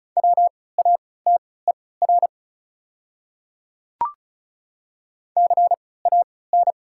Morse code, heard as a keyed tone, about 36 wpm, spaced as for 20 wpm, 690 Hz. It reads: WATER CAN
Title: 3x Character Spacing - Top 300 Words - 35wpm